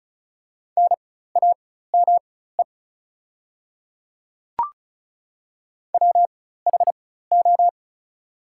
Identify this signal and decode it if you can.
Morse code, heard as a keyed tone, about 35 wpm, spaced as for 16 wpm, 700 Hz. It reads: NAME WHO
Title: 4x Character Spacing - Top 200 Words - 35wpm